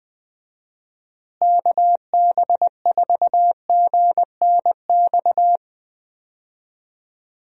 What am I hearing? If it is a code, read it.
Morse code, heard as a keyed tone, about 21 wpm, 700 Hz. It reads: KB4GNX
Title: USA Call Signs - 6 Characters - Repeated 2x - 20wpm